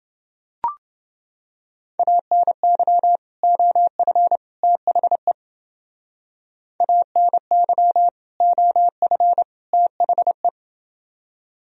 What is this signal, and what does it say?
Morse code, heard as a keyed tone, about 28 wpm, 700 Hz. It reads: ANY OF THE ANY OF THE